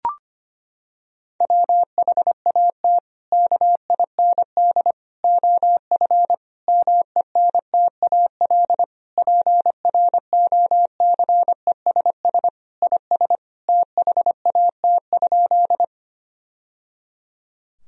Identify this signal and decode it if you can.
Morse code, heard as a keyed tone, about 25 wpm, 700 Hz. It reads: WHAT KIND OF MENTAL PROCESS IS THAT?
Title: Sentences from Top 1000 Words - Review Part 4 - 25wpm